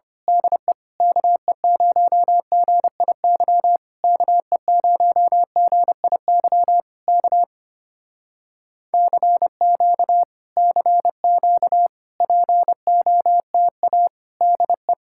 Morse code, heard as a keyed tone, about 25 wpm, 700 Hz. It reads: DE KE0GIY KE0GIY K CQ CQ POTA DE
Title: Speed-Racing Common QSO Phrases - 25wpm